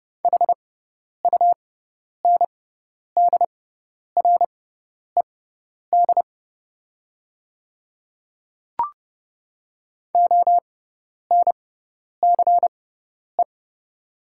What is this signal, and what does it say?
Morse code, heard as a keyed tone, about 30 wpm, spaced as for 10 wpm, 700 Hz. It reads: HUNDRED ONCE
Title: 6x Character Spacing - Top 500 Words - 30wpm